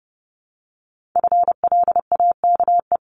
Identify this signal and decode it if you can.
Morse code, heard as a keyed tone, about 31 wpm, 700 Hz. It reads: FLAKE
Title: Multi-letter Phonemes Repeated 3x 30wpm